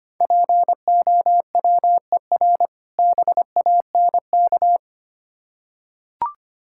Morse code, heard as a keyed tone, about 25 wpm, 700 Hz. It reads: POWER BANK
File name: practice / 25wpm / Difficult Words with Hints - 25wpm